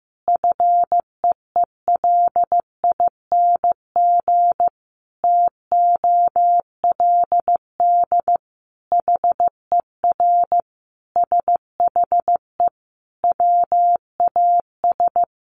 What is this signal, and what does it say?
Morse code, heard as a keyed tone, about 15 wpm, 700 Hz. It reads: FEELING TOLD HER SHE WAS
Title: Sentences from Top 1200 Words 15wpm